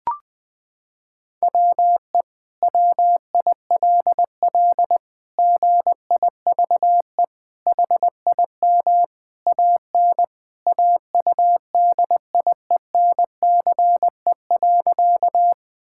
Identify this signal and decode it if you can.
Morse code, heard as a keyed tone, about 20 wpm, 700 Hz. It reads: WE WILL GIVE HIM AN AUDIENCE.
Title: Sentences from Top 1000 Words - Review Part 5 - 20wpm